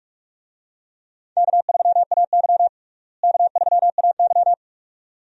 Morse code, heard as a keyed tone, about 42 wpm, 700 Hz. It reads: K3AY K3AY
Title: USA Call Signs - 4 Characters - Repeated 2x - 45wpm